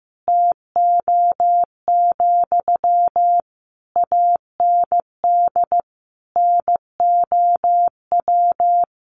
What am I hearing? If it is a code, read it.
Morse code, heard as a keyed tone, about 15 wpm, 700 Hz. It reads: TO, AND NOW
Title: Sentences from Top 600 Words 15wpm